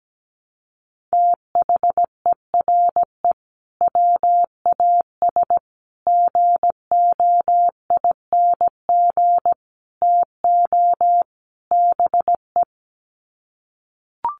Morse code, heard as a keyed tone, about 17 wpm, 700 Hz. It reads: THERE WAS GOING TO BE
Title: Sets of 5 Words Encore2 17wpm